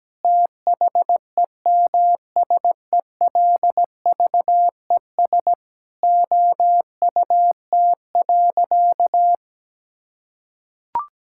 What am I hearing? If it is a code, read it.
Morse code, heard as a keyed tone, about 17 wpm, 700 Hz. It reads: THEMSELVES OUT.
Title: Sentences from Top 1000 Words - Review Part 5 - 17wpm